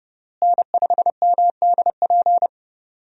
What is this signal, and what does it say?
Morse code, heard as a keyed tone, about 28 wpm, 700 Hz. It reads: N5MDP
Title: USA Call Signs - 5 Characters - Repeated 3x - 30wpm